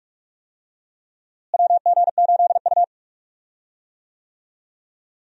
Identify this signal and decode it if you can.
Morse code, heard as a keyed tone, about 42 wpm, 700 Hz. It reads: WG8U
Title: USA Call Signs - 4 Characters - Repeated 2x - 45wpm